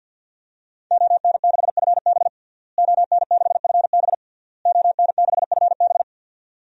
Morse code, heard as a keyed tone, about 50 wpm, 700 Hz. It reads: ON6FB ON6FB ON6FB
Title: DXCC-100 Call Signs - 5 Characters - Repeated 3x - 50wpm